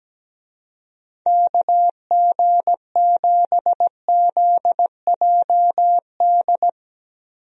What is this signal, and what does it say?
Morse code, heard as a keyed tone, about 17 wpm, 700 Hz. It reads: KG7ZJD
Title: USA Call Signs - 6 Characters - Repeated 3x - 17wpm